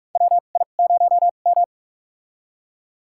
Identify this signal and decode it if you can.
Morse code, heard as a keyed tone, about 46 wpm, spaced as for 32 wpm, 700 Hz. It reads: WI0M
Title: USA Call Signs - 4 Characters - 2x Character Spacing - 45wpm